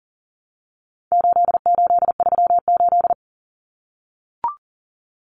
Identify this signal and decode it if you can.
Morse code, heard as a keyed tone, about 40 wpm, 700 Hz. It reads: 8838
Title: Numbers - 4-Digit 40wpm